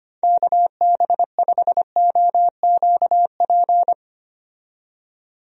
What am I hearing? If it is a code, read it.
Morse code, heard as a keyed tone, about 25 wpm, 700 Hz. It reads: KB5OQP
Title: USA Call Signs - 6 Characters - 25wpm